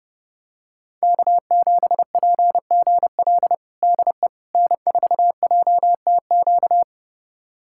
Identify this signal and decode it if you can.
Morse code, heard as a keyed tone, about 30 wpm, 700 Hz. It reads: K7PGL DE N4JTQ
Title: USA Callsign DE Callsign - 5 Characters - 30wpm